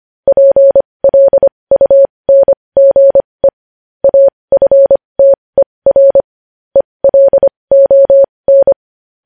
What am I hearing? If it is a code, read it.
Morse code, heard as a keyed tone, about 25 wpm, spaced as for 20 wpm, 550 Hz. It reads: PLUNGE AFTER ELON
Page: news headlines in morse code at 20 wpm